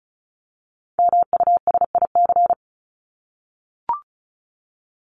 Morse code, heard as a keyed tone, about 34 wpm, 700 Hz. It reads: MUSIC
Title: Top 300 Words 35wpm